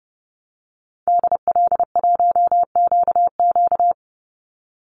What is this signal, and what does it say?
Morse code, heard as a keyed tone, about 30 wpm, 700 Hz. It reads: DL1QQ
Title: Call Signs Repeated 2x - 30wpm